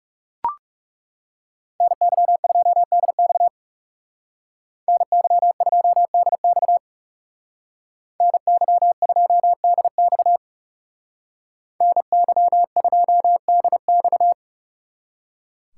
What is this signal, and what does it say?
Morse code, heard as a keyed tone, about 39 wpm, 700 Hz. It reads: NY2DX NY2DX NY2DX NY2DX